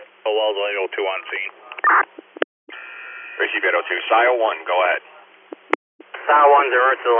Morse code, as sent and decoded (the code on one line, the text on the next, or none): none